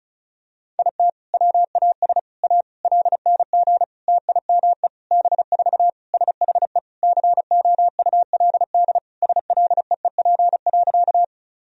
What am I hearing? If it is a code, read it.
Morse code, heard as a keyed tone, about 36 wpm, 700 Hz. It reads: IT WAS A LNG TIME B4 SHE COULD SLEEP.
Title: Sentences from Top 500 Words and Abbreviations 35wpm